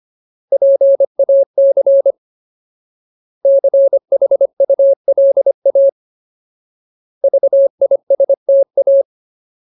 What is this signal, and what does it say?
Morse code, heard as a keyed tone, about 25 wpm, 550 Hz. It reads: PAC CHULA VISTA
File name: Ultimate Word List / 25WPM Ultimate